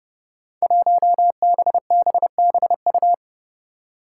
Morse code, heard as a keyed tone, about 30 wpm, 700 Hz. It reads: 1BBBU
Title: Five Challenging Letter-Number Combinations 30wpm